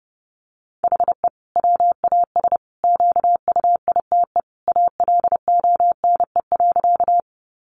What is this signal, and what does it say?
Morse code, heard as a keyed tone, about 30 wpm, 700 Hz. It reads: HE WAS QUITE ALONE.
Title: Sentences from Top 1000 Words 30wpm